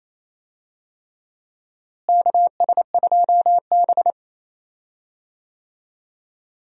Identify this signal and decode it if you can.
Morse code, heard as a keyed tone, about 28 wpm, 700 Hz. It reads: KS2B
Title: USA Call Signs - 4 Characters - Repeated 3x - 28wpm